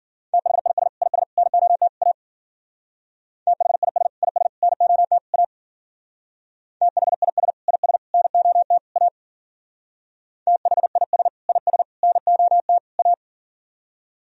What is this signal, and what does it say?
Morse code, heard as a keyed tone, about 51 wpm, 700 Hz. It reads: THIS IS NOT A THIS IS NOT A THIS IS NOT A THIS IS NOT A